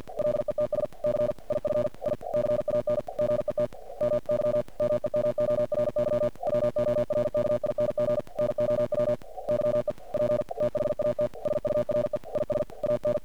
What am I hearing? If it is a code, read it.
Morse code, heard as a keyed tone, about 49 wpm, 600 Hz. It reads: LETS C IF I CAN GET MY MEMORY WORKING NOW YEP THAT SURE IS AN
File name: archived / CRSnet